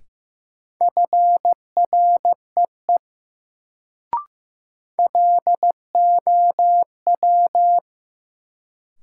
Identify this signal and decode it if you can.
Morse code, heard as a keyed tone, about 15 wpm, 700 Hz. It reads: FREE LOW
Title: Sentences from Top 500 Words - New Word Drill - 15wpm